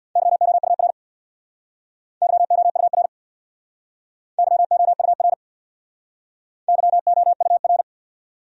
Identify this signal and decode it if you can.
Morse code, heard as a keyed tone, about 68 wpm, 700 Hz. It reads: YOUR YOUR YOUR YOUR